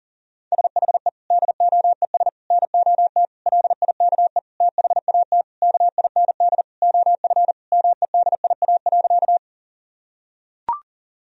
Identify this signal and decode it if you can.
Morse code, heard as a keyed tone, about 37 wpm, 700 Hz. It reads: SHE DOES NOT LIKE THAT KIND OF MEDIA.
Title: Sentences from Top 1000 Words - Review Part 4 - 40wpm